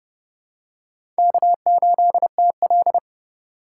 Morse code, heard as a keyed tone, about 30 wpm, 700 Hz. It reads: K8TL
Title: USA Call Signs - 4 Characters - 4x Character Spacing - 30wpm